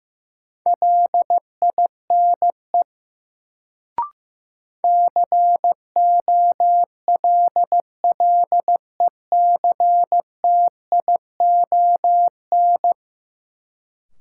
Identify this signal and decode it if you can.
Morse code, heard as a keyed tone, about 15 wpm, 700 Hz. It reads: LINE COLLECTION
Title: Sentences from Top 1200 Words - New Word Drill Alt - 15wpm